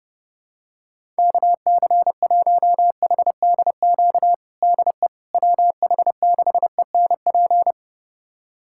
Morse code, heard as a keyed tone, about 32 wpm, 700 Hz. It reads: KC1HDQ DE WH6ENP
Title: USA Callsign DE Callsign - 6 Characters - 30wpm